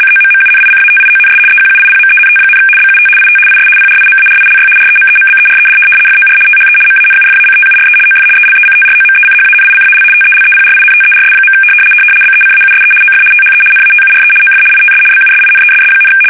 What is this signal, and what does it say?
RTTY, about 75 baud. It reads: COMMUNICATIONS EQUIPMENT, BUT IT IS PRIMARILY REMEMBERED FOR THE MANUFACTURE OF ELECTROMECHANICAL TELEPRINTERS. BECAUSE OF THE NATURE OF ITS BUSINESS,
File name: fsk850-75B / teletype-corp